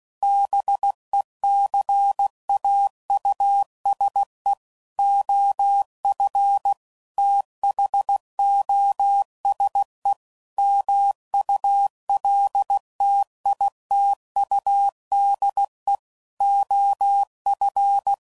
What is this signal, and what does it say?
Morse code, heard as a keyed tone, about 16 wpm, 800 Hz. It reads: BECAUSE OF THOSE MULTITUDE OF
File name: CW Code Groups @15WPM / Technical Subjects / OR Gate